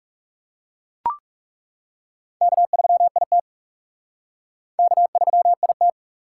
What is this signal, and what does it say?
Morse code, heard as a keyed tone, about 36 wpm, 700 Hz. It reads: K3IT K3IT
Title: Speed Racing - USA 1x2 Call Signs 30wpm